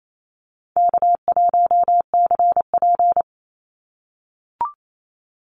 Morse code, heard as a keyed tone, about 28 wpm, 700 Hz. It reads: K1CP
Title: Easy Call Signs - 28wpm